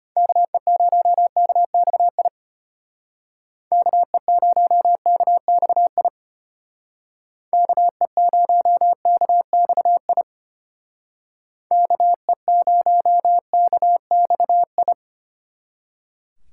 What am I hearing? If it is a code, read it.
Morse code, heard as a keyed tone, about 30 wpm, 700 Hz. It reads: KE0KXI KE0KXI KE0KXI KE0KXI
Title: Speed Racing - USA 2x3 Call Signs 25wpm